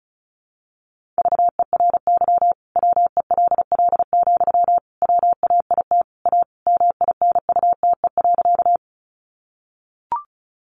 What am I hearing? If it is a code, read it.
Morse code, heard as a keyed tone, about 34 wpm, 700 Hz. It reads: VERY WELL, WAIT A MINUTE.
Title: Sentences from Top 500 Words 35wpm